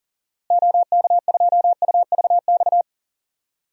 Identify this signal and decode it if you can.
Morse code, heard as a keyed tone, about 40 wpm, 700 Hz. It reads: OK2UVX
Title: DXCC-100 Call Signs - 6 Characters - Repeated 3x - 40wpm